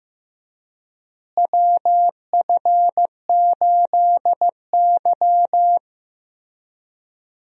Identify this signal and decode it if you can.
Morse code, heard as a keyed tone, about 16 wpm, 700 Hz. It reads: WF8Y
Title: USA Call Signs - 4 Characters - Repeated 2x - 15wpm